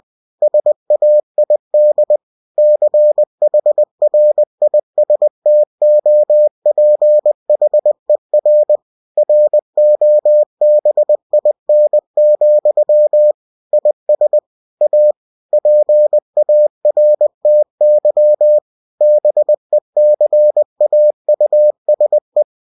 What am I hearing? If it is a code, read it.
Morse code, heard as a keyed tone, about 20 wpm, 600 Hz. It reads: SAID CHRISTOPHER ROBIN, IS A PARTY BECAUSE
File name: cw-books / winnie-the-pooh / winnie-20wpm-page-0107